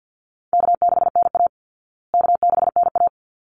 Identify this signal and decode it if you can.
Morse code, heard as a keyed tone, about 50 wpm, 700 Hz. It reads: K6NA K6NA
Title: Easy Call Signs Repeated 2x - 50wpm